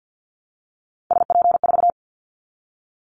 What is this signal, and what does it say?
Morse code, heard as a keyed tone, about 50 wpm, 700 Hz. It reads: SP4
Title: Callsign Prefixes for Top 25 DXCC - 50wpm